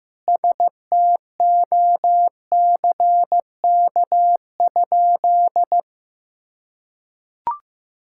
Morse code, heard as a keyed tone, about 15 wpm, 700 Hz. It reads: STOCK?
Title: Sentences from Top 1000 Words - Review Part 5 - 15wpm